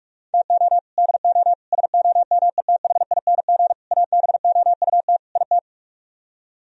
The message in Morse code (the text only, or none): TO DO SOMETHING ABOUT IT